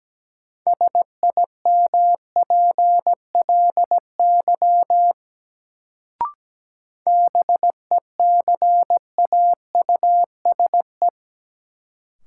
Morse code, heard as a keyed tone, about 17 wpm, 700 Hz. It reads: SIMPLY BECAUSE
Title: Sentences from Top 600 Words - New Word Drill - 17wpm